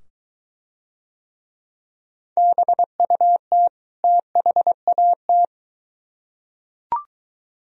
Morse code, heard as a keyed tone, about 21 wpm, 700 Hz. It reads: BUT THAT